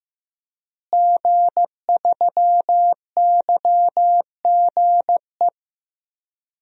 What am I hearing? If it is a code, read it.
Morse code, heard as a keyed tone, about 15 wpm, 700 Hz. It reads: G3YGE
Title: DXCC-100 Call Signs - 5 Characters - Repeated 2x - 15wpm